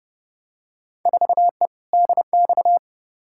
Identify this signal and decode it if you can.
Morse code, heard as a keyed tone, about 30 wpm, 700 Hz. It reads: 4E DX
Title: Field Day Exchanges Encore - 30wpm